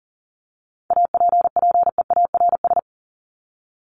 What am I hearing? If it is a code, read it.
Morse code, heard as a keyed tone, about 40 wpm, 700 Hz. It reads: APPEARS
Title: ES:S Suffix - Top 100 Words with Base Words 40wpm